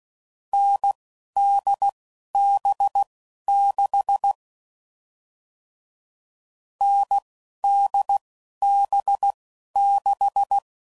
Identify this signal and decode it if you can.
Morse code, heard as a keyed tone, about 16 wpm, spaced as for 11 wpm, 800 Hz. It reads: NDB6 NDB6
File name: Mixed / Short Runs / awjndb6gz7